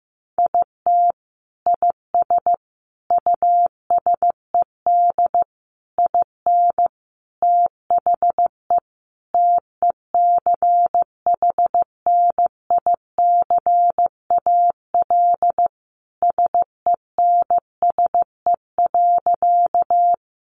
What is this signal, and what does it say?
Morse code, heard as a keyed tone, about 15 wpm, 700 Hz. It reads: IT IS USED IN THE TECHNICAL SENSE.
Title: Sentences from Top 1100 Words 15wpm